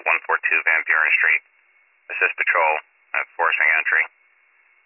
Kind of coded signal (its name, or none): Emergency Alert System (SAME) tones